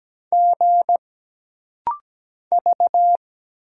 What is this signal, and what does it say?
Morse code, heard as a keyed tone, about 17 wpm, 700 Hz. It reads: G V
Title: Single Letters - Rapid-Fire 17wpm